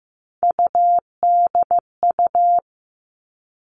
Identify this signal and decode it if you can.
Morse code, heard as a keyed tone, about 15 wpm, 700 Hz. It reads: UDU
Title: Three Challenging Letter-Number Combinations 15wpm